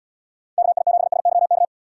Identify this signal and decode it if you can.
Morse code, heard as a keyed tone, about 72 wpm, 700 Hz. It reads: KI7SYK